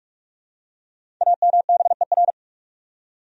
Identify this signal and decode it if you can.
Morse code, heard as a keyed tone, about 46 wpm, 700 Hz. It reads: AMBER
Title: USA Common Names - 45wpm